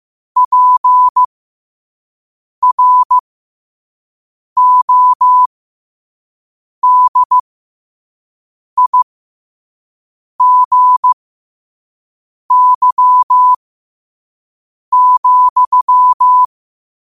Morse code, heard as a keyed tone, about 15 wpm, spaced as for 5 wpm, 1000 Hz. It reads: PRODIGY,
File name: qotdfiles / 5wpm